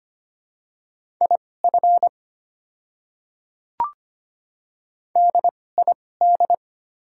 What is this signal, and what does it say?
Morse code, heard as a keyed tone, about 25 wpm, spaced as for 18 wpm, 700 Hz. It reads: IF DID
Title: 2x Character Spacing - Top 100 Words - 25wpm